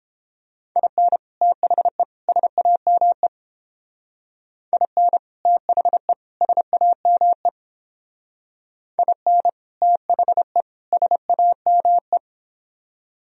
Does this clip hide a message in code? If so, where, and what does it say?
Yes: Morse code from 0.8 to 12.2 s, heard as a keyed tone, about 28 wpm, 700 Hz. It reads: IN THE SAME IN THE SAME IN THE SAME